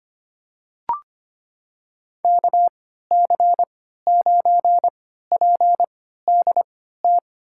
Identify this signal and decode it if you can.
Morse code, heard as a keyed tone, about 25 wpm, spaced as for 14 wpm, 700 Hz. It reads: KC9PDT
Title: USA Call Signs - 3x Character Spacing - 25wpm